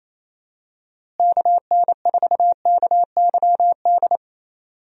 Morse code, heard as a keyed tone, about 28 wpm, 700 Hz. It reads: KN4KYD
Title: USA Call Signs - 6 Characters - Repeated 2x - 28wpm